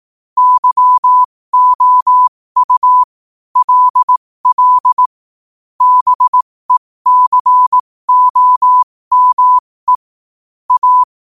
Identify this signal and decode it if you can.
Morse code, heard as a keyed tone, about 18 wpm, spaced as for 15 wpm, 1000 Hz. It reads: YOU LL BECOME A
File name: qotdfiles / 15wpm